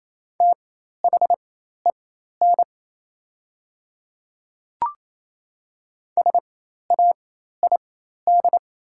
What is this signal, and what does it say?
Morse code, heard as a keyed tone, about 28 wpm, spaced as for 13 wpm, 700 Hz. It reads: THEN SAID
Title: 4x Character Spacing - Top 100 Words - 28wpm